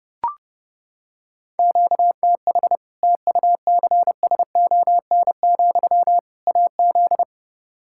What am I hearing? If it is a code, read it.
Morse code, heard as a keyed tone, about 30 wpm, 700 Hz. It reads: QTH TUCSON, AZ
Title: Practice QSO - Exchange 1 - Beginning - 30wpm